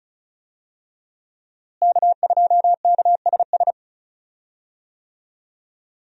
Morse code, heard as a keyed tone, about 34 wpm, 700 Hz. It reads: K2KSS
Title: USA Call Signs - 5 Characters - Repeated 2x - 35wpm